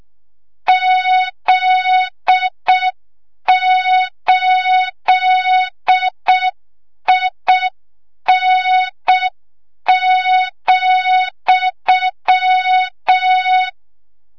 Morse code, heard as a keyed tone, about 6 wpm, 3700 Hz. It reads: Z8IN,
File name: Morze-HA2MN / hang / 25 GYAKORLAT